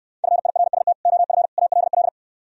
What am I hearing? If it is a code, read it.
Morse code, heard as a keyed tone, about 67 wpm, 700 Hz. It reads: FIRST OF ALL